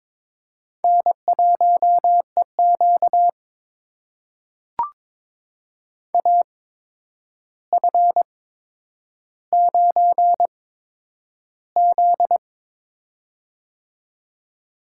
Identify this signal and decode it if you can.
Morse code, heard as a keyed tone, about 22 wpm, spaced as for 9 wpm, 700 Hz. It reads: N1EQ AF9Z
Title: USA Call Signs - 4 Characters - 8x Character Spacing - 22wpm